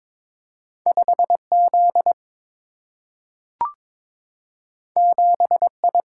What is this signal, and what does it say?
Morse code, heard as a keyed tone, about 22 wpm, 700 Hz. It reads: 5 Z 7 I
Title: Single-Letter Single-Number 22wpm